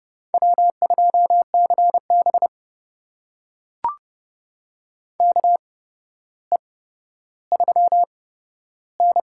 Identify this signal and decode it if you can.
Morse code, heard as a keyed tone, about 30 wpm, spaced as for 13 wpm, 700 Hz. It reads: W2CB KE3N
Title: USA Call Signs - 4 Characters - 8x Character Spacing - 30wpm